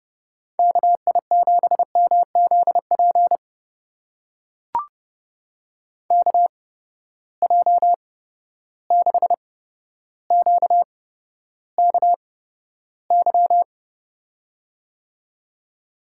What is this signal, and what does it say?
Morse code, heard as a keyed tone, about 31 wpm, spaced as for 13 wpm, 700 Hz. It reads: KI7MZP KJ6QKY
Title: USA Call Signs - 8x Character Spacing - 30wpm